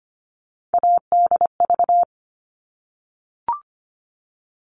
Morse code, heard as a keyed tone, about 25 wpm, 700 Hz. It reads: ADV
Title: All Cut Numbers - 3-Digit 25wpm